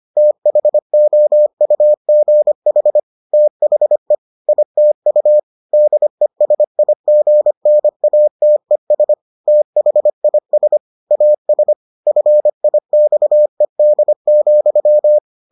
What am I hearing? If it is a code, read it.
Morse code, heard as a keyed tone, about 25 wpm, 600 Hz. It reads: THOUGH THE ITU DESIGNATES THIS AS FIXED,